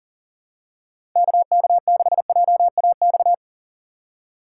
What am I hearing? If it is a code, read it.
Morse code, heard as a keyed tone, about 40 wpm, 700 Hz. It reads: KK6JAX
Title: USA Call Signs - 6 Characters - 4x Character Spacing - 40wpm